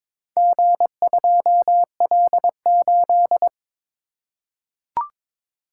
Morse code, heard as a keyed tone, about 21 wpm, 700 Hz. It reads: G2L8
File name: practice / 22wpm / Four Letter-Number Combinations - 22wpm